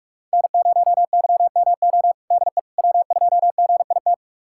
Morse code, heard as a keyed tone, about 41 wpm, 700 Hz. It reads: N0YMO DE W2GIT
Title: USA Callsign DE Callsign - 5 Characters - 45wpm